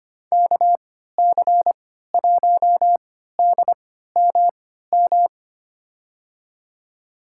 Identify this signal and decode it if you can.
Morse code, heard as a keyed tone, about 25 wpm, spaced as for 14 wpm, 700 Hz. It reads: KC1DMM
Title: USA Call Signs - 6 Characters - 3x Character Spacing - 25wpm